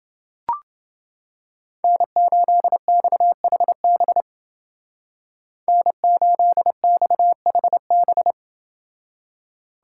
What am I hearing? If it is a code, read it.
Morse code, heard as a keyed tone, about 28 wpm, 700 Hz. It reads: N8XHB N8XHB